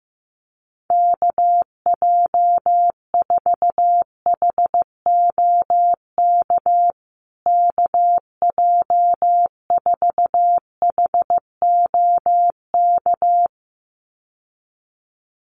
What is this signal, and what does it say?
Morse code, heard as a keyed tone, about 15 wpm, 700 Hz. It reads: KJ4HOK KJ4HOK
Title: USA Call Signs Repeated 2x 15wpm